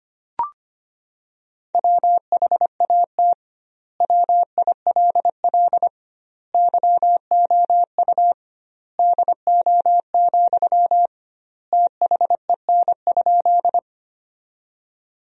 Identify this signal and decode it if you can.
Morse code, heard as a keyed tone, about 25 wpm, 700 Hz. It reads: WHAT WILL YOU DO, THEN?
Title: 2x Word-Spacing - Sentences from Top 100 Words - 25wpm